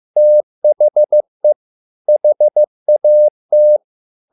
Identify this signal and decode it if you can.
Morse code, heard as a keyed tone, about 15 wpm, 600 Hz. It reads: THE HAT